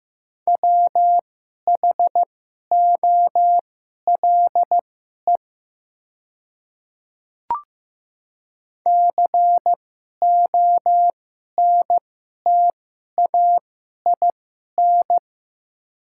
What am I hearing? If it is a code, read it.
Morse code, heard as a keyed tone, about 15 wpm, spaced as for 11 wpm, 700 Hz. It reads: WHOLE CONTAIN
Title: 2x Character Spacing - Top 500 Words - 15wpm